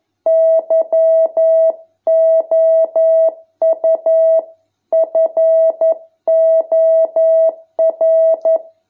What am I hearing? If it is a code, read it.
Morse code, heard as a keyed tone, about 11 wpm, 650 Hz. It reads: YOU FOR